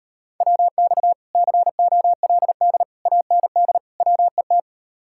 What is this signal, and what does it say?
Morse code, heard as a keyed tone, about 38 wpm, 700 Hz. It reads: WX COLD AND WET